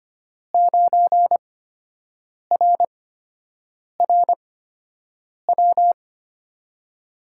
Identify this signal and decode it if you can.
Morse code, heard as a keyed tone, about 25 wpm, spaced as for 7 wpm, 700 Hz. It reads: 9RRW
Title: USA Call Signs - 8x Character Spacing - 25wpm